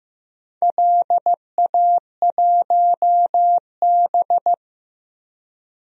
Morse code, heard as a keyed tone, about 15 wpm, 700 Hz. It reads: LA1B
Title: DXCC-100 Call Signs - 4 Characters - Repeated 2x - 15wpm